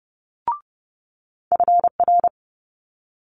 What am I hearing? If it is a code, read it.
Morse code, heard as a keyed tone, about 30 wpm, 700 Hz. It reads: FR
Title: Multi-letter Phonemes Quick Fire 30wpm